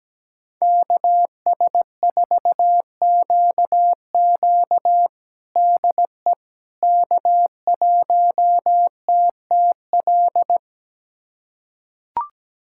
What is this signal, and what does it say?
Morse code, heard as a keyed tone, about 17 wpm, 700 Hz. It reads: KS4QQ DE K1TTL
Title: USA Callsign DE Callsign - 5 Characters - 17wpm